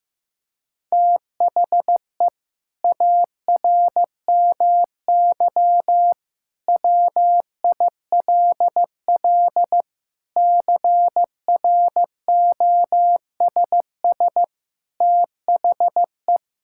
Morse code, heard as a keyed tone, about 15 wpm, 700 Hz. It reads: THE ARMY WILL CROSS THE
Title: Sentences from Top 1000 Words - Review Part 5 - 15wpm